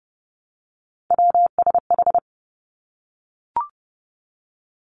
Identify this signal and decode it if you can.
Morse code, heard as a keyed tone, about 30 wpm, 700 Hz. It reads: WSH
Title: Three Challenging Letter-Number Combinations 30wpm